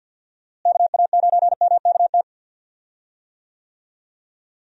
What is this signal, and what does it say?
Morse code, heard as a keyed tone, about 46 wpm, 700 Hz. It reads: KA9MKT
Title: USA Call Signs - 6 Characters - Repeated 2x - 50wpm